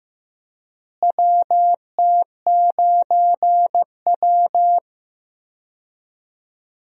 Morse code, heard as a keyed tone, about 15 wpm, 700 Hz. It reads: WT9W
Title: USA Call Signs - 4 Characters - Repeated 3x - 15wpm